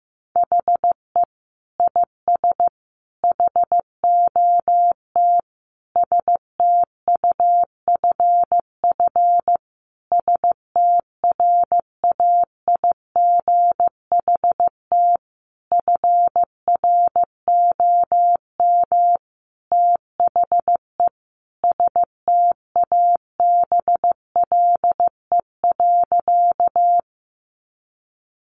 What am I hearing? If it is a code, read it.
Morse code, heard as a keyed tone, about 15 wpm, 700 Hz. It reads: HE IS HOT STUFF STRAIGHT FROM THE STABLE.
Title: Sentences from Top 1500 Words 15wpm